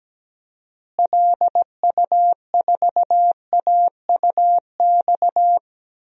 Morse code, heard as a keyed tone, about 17 wpm, 700 Hz. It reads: LU4AUX
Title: DXCC-100 Call Signs - 6 Characters - Repeated 2x - 17wpm